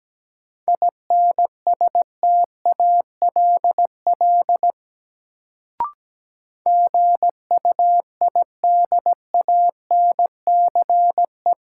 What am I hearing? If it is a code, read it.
Morse code, heard as a keyed tone, about 17 wpm, 700 Hz. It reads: INSTALL GUIDANCE
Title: Sentences from Top 1600 Words - New Word Drill Alt - 17wpm